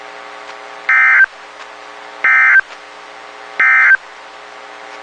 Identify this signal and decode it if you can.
Emergency Alert System (SAME) tones, heard as harsh buzzing bursts. It is heard from 0.8 to 4.0 s.
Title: WXK91 NOAA Weather Radio